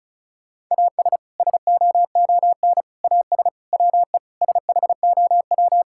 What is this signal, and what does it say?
Morse code, heard as a keyed tone, about 34 wpm, 700 Hz. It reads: AS SOON AS WE SHOW
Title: Sentences from Top 1600 Words 35wpm